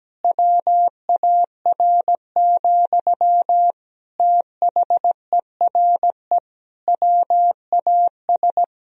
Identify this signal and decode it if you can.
Morse code, heard as a keyed tone, about 17 wpm, 700 Hz. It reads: WAR, THERE WAS
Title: Sentences from Top 1000 Words - Review Part 2 - 17wpm